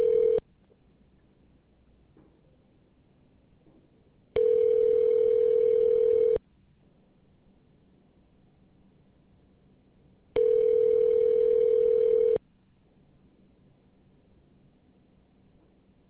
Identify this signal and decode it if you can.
Ringback tone (North America), 440 + 480 Hz.